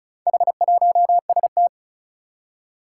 Morse code, heard as a keyed tone, about 36 wpm, 690 Hz. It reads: H1ST
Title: Four Challenging Letter-Number Combinations 35wpm